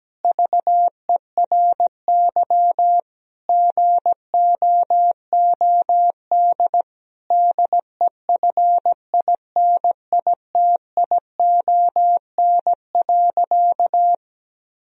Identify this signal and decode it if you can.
Morse code, heard as a keyed tone, about 17 wpm, 700 Hz. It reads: VERY GOOD DEFINITION.
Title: Sentences from Top 1000 Words - Review Part 4 - 17wpm